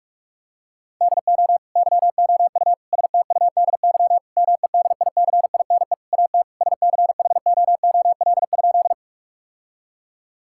Morse code, heard as a keyed tone, about 46 wpm, 700 Hz. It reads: DO YOU STUDY MEDICINE AT SCHOOL?
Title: Sentences from Top 1000 Words - Review Part 6 - 45wpm